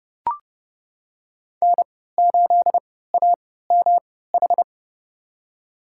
Morse code, heard as a keyed tone, about 30 wpm, spaced as for 17 wpm, 700 Hz. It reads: N8AMH
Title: USA Call Signs - 5 Characters - 3x Character Spacing - 30wpm